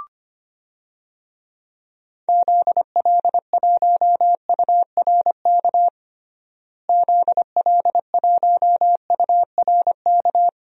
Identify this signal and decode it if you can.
Morse code, heard as a keyed tone, about 25 wpm, 700 Hz. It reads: ZL1URK ZL1URK
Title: DXCC-100 Call Signs - 6 Characters - Repeated 3x - 25wpm